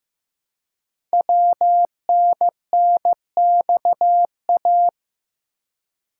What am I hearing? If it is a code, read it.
Morse code, heard as a keyed tone, about 15 wpm, 700 Hz. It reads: WNNXA
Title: Code Groups - 5 Letters - 15wpm